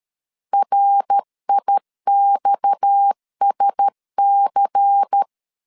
Morse code, heard as a keyed tone, about 13 wpm, 800 Hz. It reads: RIXSC
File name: CW / DET / 1 9 1500C S B 12wpm 3819